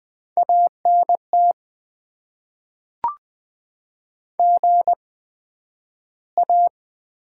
Morse code, heard as a keyed tone, about 20 wpm, spaced as for 11 wpm, 700 Hz. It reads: ANT GA
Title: Top 90 QSO Elements - 8x Character Spacing - 20wpm